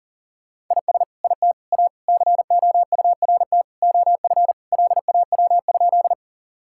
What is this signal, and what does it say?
Morse code, heard as a keyed tone, about 37 wpm, 700 Hz. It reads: IS IT A COURT OF LAW?
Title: Sentences from Top 1600 Words 40wpm